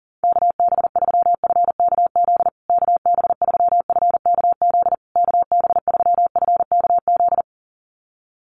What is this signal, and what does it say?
Morse code, heard as a keyed tone, about 40 wpm, 700 Hz. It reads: KB3FKZ KB3FKZ KB3FKZ
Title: USA Call Signs Repeated 3x 40wpm